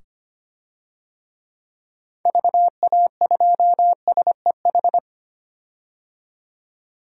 Morse code, heard as a keyed tone, about 25 wpm, 700 Hz. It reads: VA2SEH
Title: DXCC-100 Call Signs - 6 Characters - 25wpm